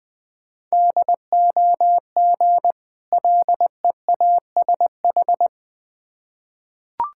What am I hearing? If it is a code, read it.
Morse code, heard as a keyed tone, about 20 wpm, 700 Hz. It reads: DOG LEASH
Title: Difficult Words with Hints - 20wpm